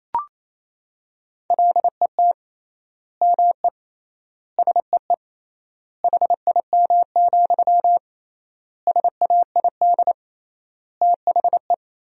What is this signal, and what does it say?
Morse code, heard as a keyed tone, about 28 wpm, 700 Hz. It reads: LET ME SEE HIM, SAID THE
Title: 3x Word-Spacing - Sentences from Top 400 Words - 28wpm